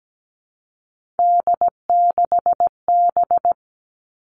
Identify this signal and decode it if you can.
Morse code, heard as a keyed tone, about 17 wpm, 700 Hz. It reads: D6B
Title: Three Challenging Letter-Number Combinations 17wpm